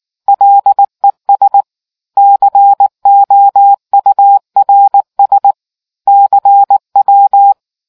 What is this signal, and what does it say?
Morse code, heard as a keyed tone, about 18 wpm, 800 Hz. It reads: LES COURS CW